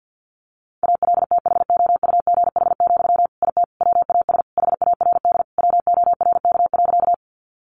Morse code, heard as a keyed tone, about 48 wpm, 700 Hz. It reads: ALTHOUGH, IT WAS HARD WORK.
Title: Sentences from Top 600 Words 50wpm